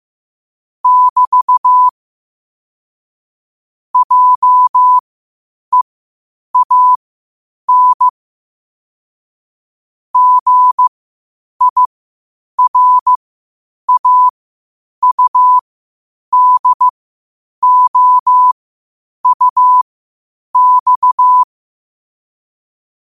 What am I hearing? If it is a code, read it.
Morse code, heard as a keyed tone, about 15 wpm, spaced as for 8 wpm, 1000 Hz. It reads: <BT> JEAN GIRAUDOUX
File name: qotdfiles / 8wpm